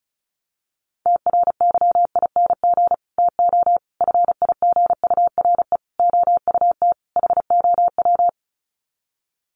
Morse code, heard as a keyed tone, about 34 wpm, 700 Hz. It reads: TRYING TO FIGURE OUT HOW
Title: Sets of 5 Words Encore 35wpm